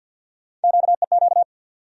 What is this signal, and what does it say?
Morse code, heard as a keyed tone, about 51 wpm, 700 Hz. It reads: QEQ
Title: USA Call Sign Postfixes - 3 Letters 50wpm